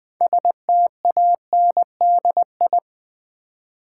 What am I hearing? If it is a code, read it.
Morse code, heard as a keyed tone, about 20 wpm, 690 Hz. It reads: STANDI
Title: Build A Word - 3 Letters At A Time - 8-Letter Words - 20wpm